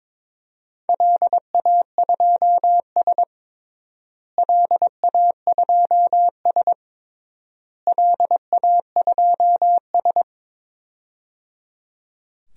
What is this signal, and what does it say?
Morse code, heard as a keyed tone, about 22 wpm, 700 Hz. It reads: LA2S LA2S LA2S
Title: DXCC-100 Call Signs - 4 Characters - Repeated 3x - 22wpm